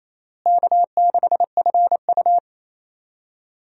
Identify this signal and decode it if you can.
Morse code, heard as a keyed tone, about 28 wpm, 700 Hz. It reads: K6FU
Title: USA Call Signs - 4 Characters - Repeated 3x - 28wpm